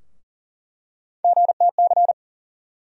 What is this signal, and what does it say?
Morse code, heard as a keyed tone, about 40 wpm, 700 Hz. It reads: GTC
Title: USA Call Sign Postfixes - 3 Letters 40wpm